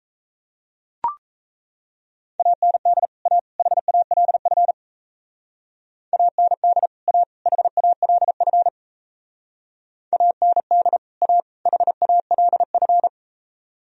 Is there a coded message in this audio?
Yes: Morse code from 2.4 to 13.1 s, heard as a keyed tone, about 37 wpm, 700 Hz. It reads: AND A HALF AND A HALF AND A HALF